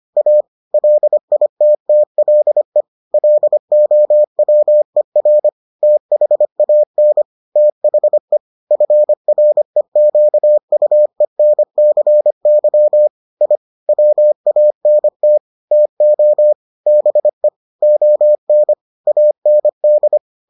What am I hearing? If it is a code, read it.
Morse code, heard as a keyed tone, about 25 wpm, 600 Hz. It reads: A LITTLE LOWER THAN THE FREQUENCY I WANT TO BE ON AND